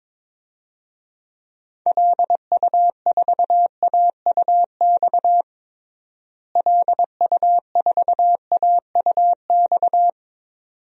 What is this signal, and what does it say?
Morse code, heard as a keyed tone, about 22 wpm, 700 Hz. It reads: LU4AUX LU4AUX
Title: DXCC-100 Call Signs - 6 Characters - Repeated 2x - 22wpm